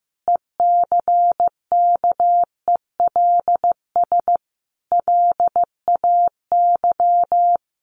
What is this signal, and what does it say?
Morse code, heard as a keyed tone, about 15 wpm, 700 Hz. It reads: ECKELS LAY
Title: A Sound Of Thunder 15wpm